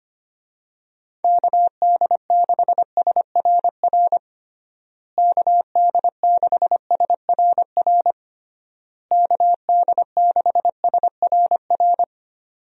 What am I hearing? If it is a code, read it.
Morse code, heard as a keyed tone, about 25 wpm, 700 Hz. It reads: KD6SRR KD6SRR KD6SRR
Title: USA Call Signs - 6 Characters - Repeated 3x - 25wpm